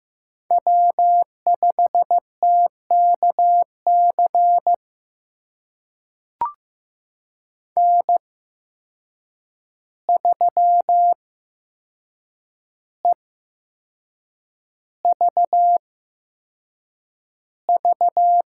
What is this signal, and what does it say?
Morse code, heard as a keyed tone, about 15 wpm, spaced as for 6 wpm, 700 Hz. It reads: W5TKC N3EVV
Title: USA Call Signs - 5 Characters - 8x Character Spacing - 15wpm